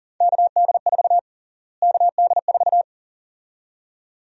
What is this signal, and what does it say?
Morse code, heard as a keyed tone, about 40 wpm, 700 Hz. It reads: KD4 KD4
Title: USA Call Sign Prefixes - Repeated 2x - 40wpm